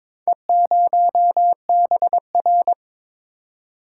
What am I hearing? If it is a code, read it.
Morse code, heard as a keyed tone, about 22 wpm, 700 Hz. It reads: E0BR
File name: practice / 22wpm / Four Letter-Number Combinations - 22wpm